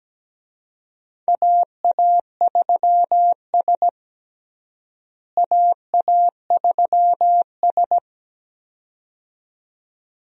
Morse code, heard as a keyed tone, about 17 wpm, 700 Hz. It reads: AA3S AA3S
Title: USA Call Signs - 4 Characters - Repeated 2x - 17wpm